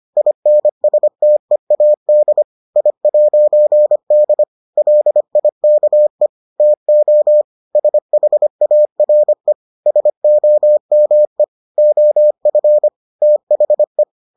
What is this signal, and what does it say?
Morse code, heard as a keyed tone, about 25 wpm, 600 Hz. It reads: INSTEAD I'D LIKE TO SHARE SOME OF THE